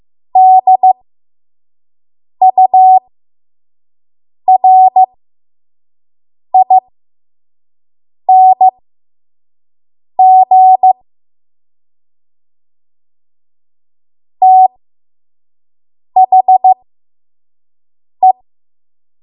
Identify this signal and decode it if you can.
Morse code, heard as a keyed tone, about 15 wpm, spaced as for 5 wpm, 750 Hz. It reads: DURING THE
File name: Morse / 05WPM